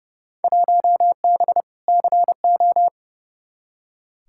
Morse code, heard as a keyed tone, about 30 wpm, 700 Hz. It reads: 1B CO
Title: Field Day Exchanges - 30wpm